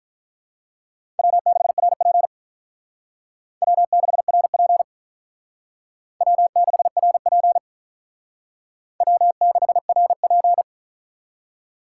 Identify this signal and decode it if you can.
Morse code, heard as a keyed tone, about 45 wpm, 700 Hz. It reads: W6RP W6RP W6RP W6RP